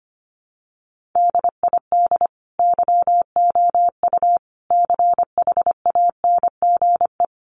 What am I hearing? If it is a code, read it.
Morse code, heard as a keyed tone, about 25 wpm, 700 Hz. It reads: DID YOU CHANGE
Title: Sentences from Top 1400 Words 25wpm